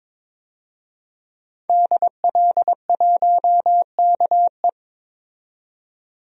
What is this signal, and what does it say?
Morse code, heard as a keyed tone, about 22 wpm, 700 Hz. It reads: DL1KE
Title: DXCC-100 Call Signs - 5 Characters - 22wpm